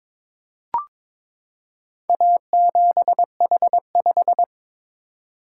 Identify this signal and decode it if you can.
Morse code, heard as a keyed tone, about 22 wpm, 690 Hz. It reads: A7H5
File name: practice / 22wpm / Four Letter-Number Combinations - 22wpm